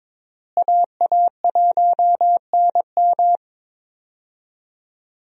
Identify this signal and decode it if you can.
Morse code, heard as a keyed tone, about 22 wpm, 700 Hz. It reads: AA1NM
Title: USA Call Signs - 5 Characters - Repeated 3x - 22wpm